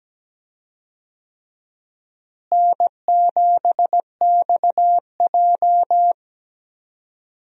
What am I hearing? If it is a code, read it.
Morse code, heard as a keyed tone, about 17 wpm, 700 Hz. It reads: N7XJ
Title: USA Call Signs - 4 Characters - Repeated 3x - 17wpm